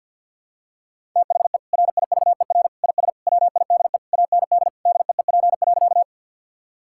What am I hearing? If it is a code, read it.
Morse code, heard as a keyed tone, about 52 wpm, 700 Hz. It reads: THE RIVER IS WIDE AND DEEP.
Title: Sentences from Top 1000 Words - Review Part 6 - 50wpm